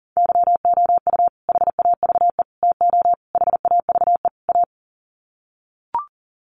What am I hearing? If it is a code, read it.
Morse code, heard as a keyed tone, about 37 wpm, 700 Hz. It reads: YOU HAVE TO HAVE A
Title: Sets of 5 Words Encore 40wpm